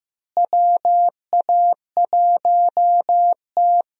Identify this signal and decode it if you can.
Morse code, heard as a keyed tone, about 15 wpm, 700 Hz. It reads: WA1T
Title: USA Call Signs - 4 Characters - Repeated 3x - 15wpm